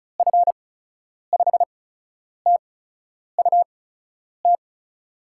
Morse code, heard as a keyed tone, about 36 wpm, spaced as for 10 wpm, 700 Hz. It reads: F5TUT
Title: USA Call Signs - 6 Characters - 8x Character Spacing - 35wpm